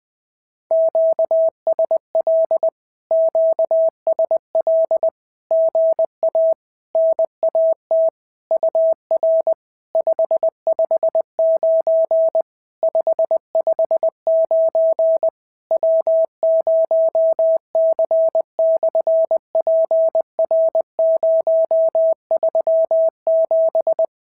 Morse code, heard as a keyed tone, about 20 wpm, 650 Hz. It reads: QSL QSL GA NAT UR 559 559 W0C/PR037
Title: SOTA QSO Practice 20wpm